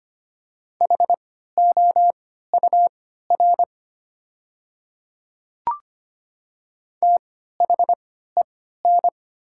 Morse code, heard as a keyed tone, about 25 wpm, spaced as for 14 wpm, 700 Hz. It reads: HOUR THEN
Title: 3x Character Spacing - Top 500 Words - 25wpm